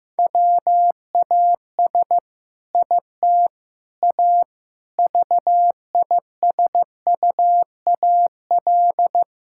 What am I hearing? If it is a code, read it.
Morse code, heard as a keyed tone, about 15 wpm, 700 Hz. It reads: WAS IT A VISUAL
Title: Sentences from Top 1000 Words - Review Part 6 - 15wpm